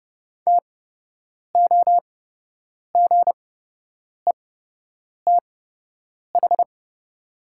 Morse code, heard as a keyed tone, about 30 wpm, spaced as for 8 wpm, 690 Hz. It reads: TOGETH
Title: 8x Character Spacing - Top 300 Words - 30wpm